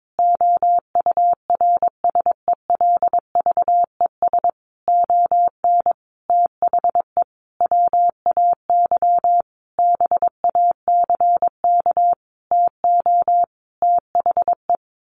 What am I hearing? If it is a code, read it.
Morse code, heard as a keyed tone, about 22 wpm, 700 Hz. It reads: OURSELVES ON THE WAY BACK TO THE
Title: A Sound Of Thunder 22wpm